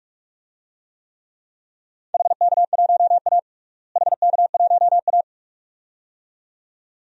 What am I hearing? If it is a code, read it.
Morse code, heard as a keyed tone, about 41 wpm, 700 Hz. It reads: HK1A HK1A
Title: DXCC-100 Call Signs - 4 Characters - Repeated 2x - 45wpm